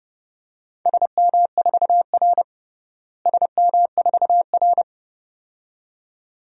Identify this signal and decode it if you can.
Morse code, heard as a keyed tone, about 31 wpm, 700 Hz. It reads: SM4R SM4R
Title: DXCC-100 Call Signs - 4 Characters - Repeated 2x - 30wpm